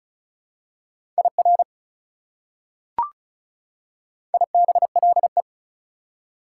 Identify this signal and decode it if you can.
Morse code, heard as a keyed tone, about 36 wpm, 700 Hz. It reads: IR IBLE
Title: Prefixes and Suffixes 35wpm